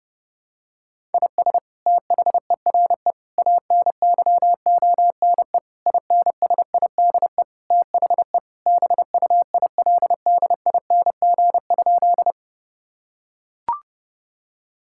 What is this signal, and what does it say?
Morse code, heard as a keyed tone, about 31 wpm, 700 Hz. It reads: IS THERE ANYONE INSIDE THE BUILDING?
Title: Sentences from Top 1000 Words - Review Part 6 - 30wpm